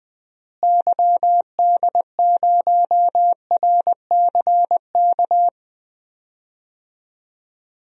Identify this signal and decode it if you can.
Morse code, heard as a keyed tone, about 20 wpm, 700 Hz. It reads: YD0RCK
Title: DXCC-100 Call Signs - 6 Characters - Repeated 3x - 20wpm